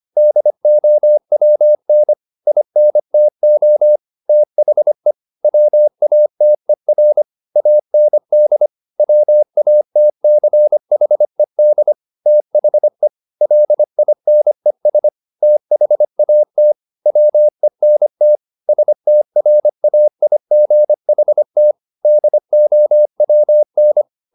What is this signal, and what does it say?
Morse code, heard as a keyed tone, about 25 wpm, 600 Hz. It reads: DOWN INTO THE WATER AND WATCHED THE LINES THAT WENT STRAIGHT DOWN
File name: the old man and the sea part 1 25wpm